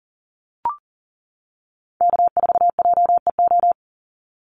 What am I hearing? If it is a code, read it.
Morse code, heard as a keyed tone, about 40 wpm, 700 Hz. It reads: K4JEO
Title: USA Call Signs 40wpm